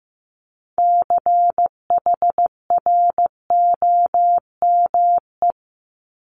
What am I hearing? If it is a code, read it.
Morse code, heard as a keyed tone, about 15 wpm, 700 Hz. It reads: CHROME
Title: Multi-letter Phonemes Repeated 3x 15wpm